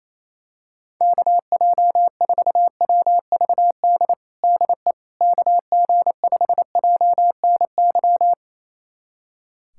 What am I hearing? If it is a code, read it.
Morse code, heard as a keyed tone, about 28 wpm, 700 Hz. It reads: KJ4WVD DE KG5JNY
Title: USA Callsign DE Callsign - 6 Characters - 28wpm